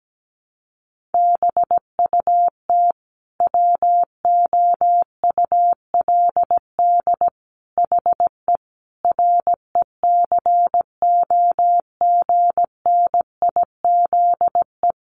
Morse code, heard as a keyed tone, about 17 wpm, 700 Hz. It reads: BUT WOULD HE RECOGNIZE
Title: Sentences from Top 1400 Words 17wpm